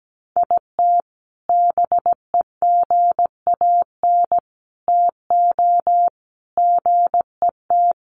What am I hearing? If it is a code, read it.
Morse code, heard as a keyed tone, about 17 wpm, 700 Hz. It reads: IT BEGAN TO GET
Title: Sentences from Top 1400 Words 17wpm